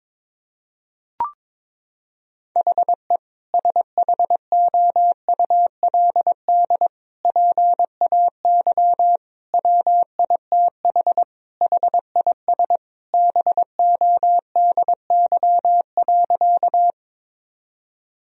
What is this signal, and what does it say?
Morse code, heard as a keyed tone, about 22 wpm, 700 Hz. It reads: HE SHOULD PAY WITH HIS BODY.
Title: Sentences from Top 1000 Words - Review Part 2 - 22wpm